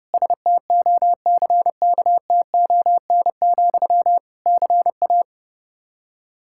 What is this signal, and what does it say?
Morse code, heard as a keyed tone, about 30 wpm, 700 Hz. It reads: STOCKTON, CA
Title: Practice QSO - Exchange 1 - Beginning - 30wpm